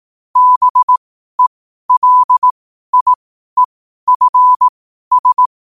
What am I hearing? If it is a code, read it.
Morse code, heard as a keyed tone, about 18 wpm, spaced as for 12 wpm, 1000 Hz. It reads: BELIEFS
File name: 12wpm